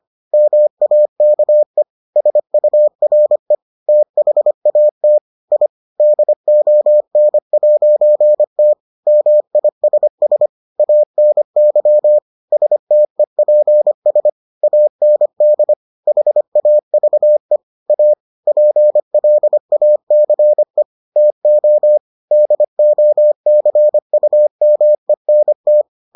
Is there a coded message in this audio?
Yes: Morse code from 0.3 to 25.8 s, heard as a keyed tone, about 25 wpm, 600 Hz. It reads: MAKE SURE THAT I DON'T MISS ANY STEPS AND HAVE A PLACE TO DOCUMENT